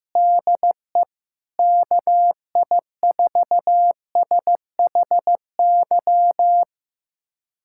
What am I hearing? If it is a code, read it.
Morse code, heard as a keyed tone, about 15 wpm, 700 Hz. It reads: DE KI4SHY
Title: USA Callsign DE Callsign - 6 Characters - 15wpm